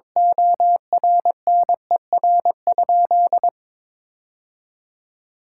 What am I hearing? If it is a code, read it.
Morse code, heard as a keyed tone, about 22 wpm, 700 Hz. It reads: ORNER?
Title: Sentences from Top 1600 Words 22wpm